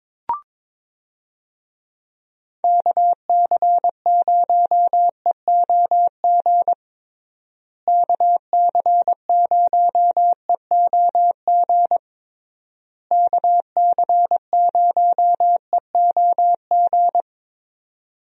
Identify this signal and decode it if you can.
Morse code, heard as a keyed tone, about 22 wpm, 700 Hz. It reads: KC0EOG KC0EOG KC0EOG
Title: USA Call Signs - 6 Characters - Repeated 3x - 22wpm